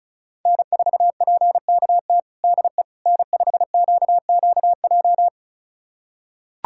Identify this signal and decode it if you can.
Morse code, heard as a keyed tone, about 34 wpm, 700 Hz. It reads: N4PKT DE N5QQJ
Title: USA Callsign DE Callsign - 5 Characters - 35wpm